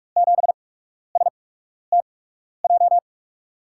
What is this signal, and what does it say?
Morse code, heard as a keyed tone, about 44 wpm, spaced as for 12 wpm, 700 Hz. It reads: 7STJ
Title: USA Call Signs - 6 Characters - 8x Character Spacing - 45wpm